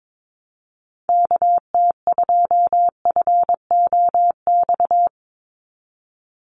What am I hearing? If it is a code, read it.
Morse code, heard as a keyed tone, about 22 wpm, 700 Hz. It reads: KT2FOX
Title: USA Call Signs 22wpm